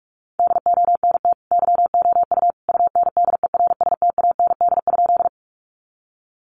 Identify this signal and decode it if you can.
Morse code, heard as a keyed tone, about 44 wpm, 700 Hz. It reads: DONT YOU UNDERSTAND?
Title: Sentences from Top 600 Words 45wpm